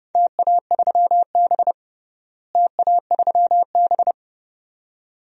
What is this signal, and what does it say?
Morse code, heard as a keyed tone, about 30 wpm, 700 Hz. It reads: TA3B TA3B
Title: DXCC-100 Call Signs - 4 Characters - Repeated 2x - 30wpm